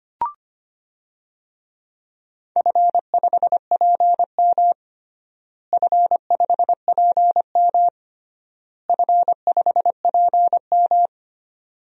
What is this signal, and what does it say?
Morse code, heard as a keyed tone, about 25 wpm, 700 Hz. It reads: F5PM F5PM F5PM
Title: DXCC-100 Call Signs - 4 Characters - Repeated 3x - 25wpm